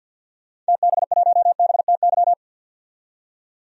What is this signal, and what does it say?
Morse code, heard as a keyed tone, about 50 wpm, 700 Hz. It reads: TB1BTY
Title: DXCC-100 Call Signs - 6 Characters - 50wpm